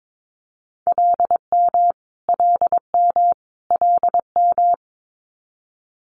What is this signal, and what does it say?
Morse code, heard as a keyed tone, about 22 wpm, 700 Hz. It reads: LM LM LM
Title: Multi-letter Phonemes Repeated 3x 22wpm